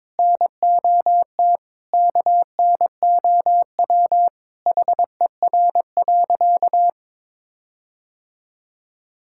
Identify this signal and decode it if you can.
Morse code, heard as a keyed tone, about 21 wpm, 700 Hz. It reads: NOT KNOW HER.
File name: practice / 22wpm / Sentences with 4-Word Phrases - 22wpm